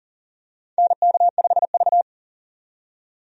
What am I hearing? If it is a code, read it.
Morse code, heard as a keyed tone, about 40 wpm, 700 Hz. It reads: NK5V
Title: USA Call Signs - 4 Characters - 40wpm